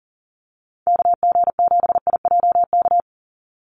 Morse code, heard as a keyed tone, about 40 wpm, 690 Hz. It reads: KG7IJK
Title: USA Call Signs 40wpm